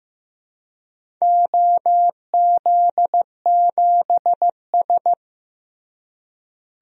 Morse code, heard as a keyed tone, about 15 wpm, 700 Hz. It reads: OZ7S
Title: DXCC-100 Call Signs - 4 Characters - Repeated 2x - 15wpm